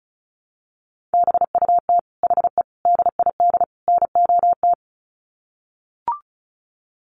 Morse code, heard as a keyed tone, about 34 wpm, 700 Hz. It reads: BUT HE DID NOT
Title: Sets of 4 Words 35wpm